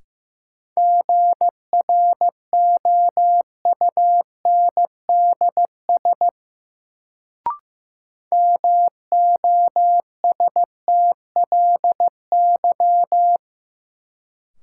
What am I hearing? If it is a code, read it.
Morse code, heard as a keyed tone, about 15 wpm, 700 Hz. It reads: GROUNDS MOSTLY
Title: Sentences from Top 1300 Words - New Word Drill - 15wpm